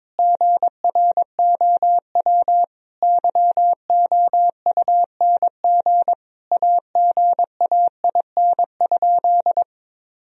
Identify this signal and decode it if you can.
Morse code, heard as a keyed tone, about 22 wpm, 700 Hz. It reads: GROW YOUNG AGAIN?
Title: Sentences from Top 1000 Words - Review Part 1 - 22wpm